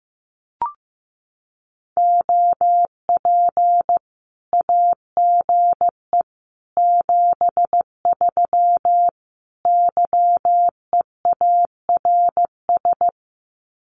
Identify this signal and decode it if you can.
Morse code, heard as a keyed tone, about 15 wpm, 700 Hz. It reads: OP AGE 73 YEARS
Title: Common QSO Phrases 15wpm